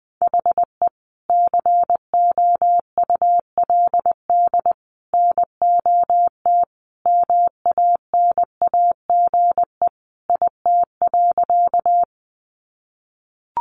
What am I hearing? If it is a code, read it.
Morse code, heard as a keyed tone, about 19 wpm, 700 Hz. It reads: HE COULD NOT MANAGE IT.
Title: Sentences from Top 1400 Words 20wpm